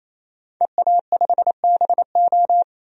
Morse code, heard as a keyed tone, about 28 wpm, 700 Hz. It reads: EA5BO
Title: DXCC-100 Call Signs - 5 Characters - 28wpm